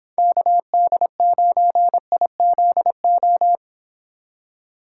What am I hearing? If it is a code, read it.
Morse code, heard as a keyed tone, about 29 wpm, 700 Hz. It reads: KD9IZO